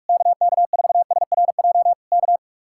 Morse code, heard as a keyed tone, about 44 wpm, 700 Hz. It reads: KK4SRJ K